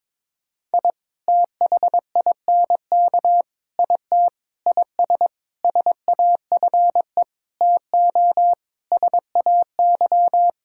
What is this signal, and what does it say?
Morse code, heard as a keyed tone, about 22 wpm, 700 Hz. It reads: I THINK IT IS SAFE TO SAY
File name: Sentences with 7-Word Phrases - 22wpm